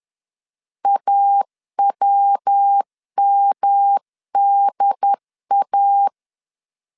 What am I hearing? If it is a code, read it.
Morse code, heard as a keyed tone, about 10 wpm, 800 Hz. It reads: AWMDA
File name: 10wpm